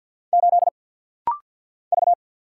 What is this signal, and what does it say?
Morse code, heard as a keyed tone, about 48 wpm, 700 Hz. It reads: 8 V
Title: Single Letter-Number - Rapid-Fire 50wpm